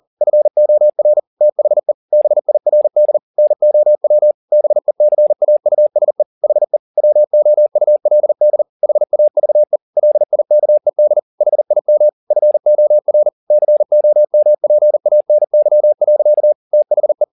Morse code, heard as a keyed tone, about 40 wpm, 600 Hz. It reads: FOR THE BIRD NOW BECAUSE HE WOULD HAVE LIKED HIM FOR COMPANY. THE